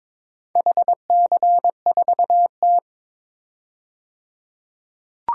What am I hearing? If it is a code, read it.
Morse code, heard as a keyed tone, about 22 wpm, 700 Hz. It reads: HC4T
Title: DXCC-100 Call Signs - 4 Characters - 22wpm